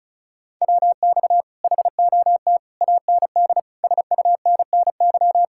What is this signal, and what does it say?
Morse code, heard as a keyed tone, about 40 wpm, 700 Hz. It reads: WX HOT AND SUNNY